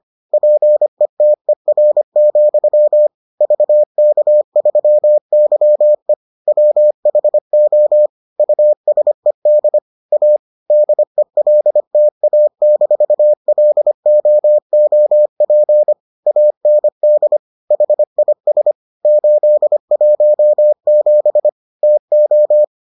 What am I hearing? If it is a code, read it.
Morse code, heard as a keyed tone, about 25 wpm, 600 Hz. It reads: PETER, VK3YE WHO USED A DELTA-LOOP AND HIS 817 TO